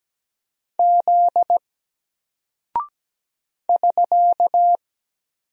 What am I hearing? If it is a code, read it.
Morse code, heard as a keyed tone, about 17 wpm, 700 Hz. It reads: Z <SK>
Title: Single Character (Letter:Number:Punctuation:Prosign) Alt 17wpm